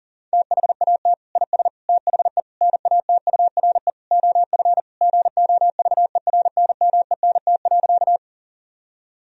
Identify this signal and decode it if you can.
Morse code, heard as a keyed tone, about 40 wpm, 700 Hz. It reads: THAT IS THE NATURE OF GOVERNMENT.
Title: Sentences from Top 1000 Words - Review Part 3 - 40wpm